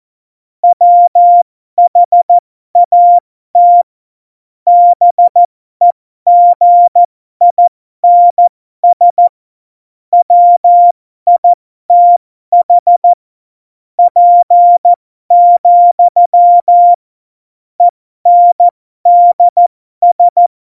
Morse code, heard as a keyed tone, about 14 wpm, spaced as for 12 wpm, 700 Hz. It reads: WHAT BEGINS WITH P, ENDS
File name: Morsekurs / Witze